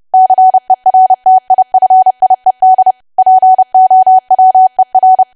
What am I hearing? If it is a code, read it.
Morse code, heard as a keyed tone, about 30 wpm, 750 Hz. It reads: CERTIFIED POWER
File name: cwa / W1AW-Level3 / 30WPM